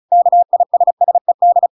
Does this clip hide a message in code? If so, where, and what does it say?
Yes: Morse code from 0.1 to 1.7 s, heard as a keyed tone, about 34 wpm, 700 Hz. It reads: KISSED